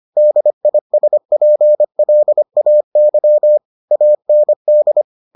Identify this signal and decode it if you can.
Morse code, heard as a keyed tone, about 25 wpm, 600 Hz. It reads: DISPLAY AND